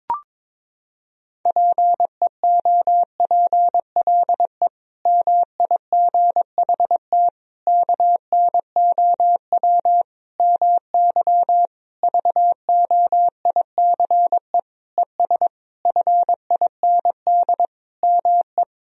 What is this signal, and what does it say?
Morse code, heard as a keyed tone, about 22 wpm, 700 Hz. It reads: PEOPLE MIGHT KNOW MY VOICE ES FIND ME
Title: Sentences from Top 500 Words and Abbreviations 22wpm